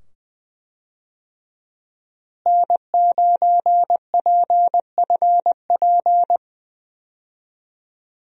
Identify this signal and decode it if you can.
Morse code, heard as a keyed tone, about 20 wpm, 700 Hz. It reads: N9PFP
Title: USA Call Signs - 5 Characters - Repeated 2x - 20wpm